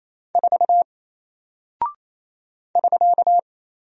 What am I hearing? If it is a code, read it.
Morse code, heard as a keyed tone, about 28 wpm, 700 Hz. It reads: 4 <SK>
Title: Single Character (Letter:Number:Punctuation:Prosign) 28wpm